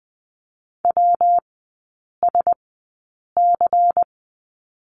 Morse code, heard as a keyed tone, about 20 wpm, spaced as for 8 wpm, 700 Hz. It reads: WSC
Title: 178 Lesson 29e - Slash - Three Characters 20wpm